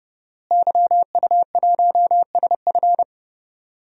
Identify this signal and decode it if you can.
Morse code, heard as a keyed tone, about 32 wpm, 700 Hz. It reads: YU1SF
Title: DXCC-100 Call Signs - 5 Characters - Repeated 2x - 30wpm